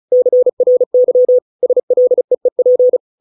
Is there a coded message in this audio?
Yes: Morse code from 0.1 to 3.0 s, heard as a keyed tone, about 36 wpm, 500 Hz. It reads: CRY SLEEP